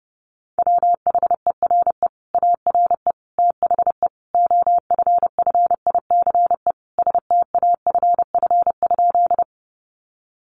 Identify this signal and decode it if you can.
Morse code, heard as a keyed tone, about 30 wpm, 700 Hz. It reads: WHERE ARE THE OFFICE STAFF?
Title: Sentences from Top 1200 Words 30wpm